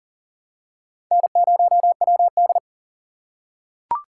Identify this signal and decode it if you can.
Morse code, heard as a keyed tone, about 37 wpm, 700 Hz. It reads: N0WD
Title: USA Call Signs - 4 Characters - 3x Character Spacing - 40wpm